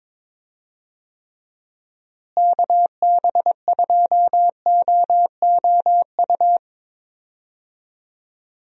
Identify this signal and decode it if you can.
Morse code, heard as a keyed tone, about 21 wpm, 700 Hz. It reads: KB2OOU
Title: USA Call Signs - 6 Characters - Repeated 2x - 22wpm